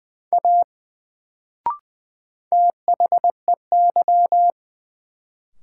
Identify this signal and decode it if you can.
Morse code, heard as a keyed tone, about 20 wpm, 700 Hz. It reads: A THEY
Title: Sentences from Top 100 Words - New Word Drill - 20wpm